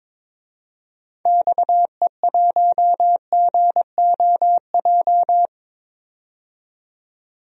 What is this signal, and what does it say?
Morse code, heard as a keyed tone, about 22 wpm, 700 Hz. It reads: XE1GOJ
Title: DXCC-100 Call Signs - 6 Characters - 22wpm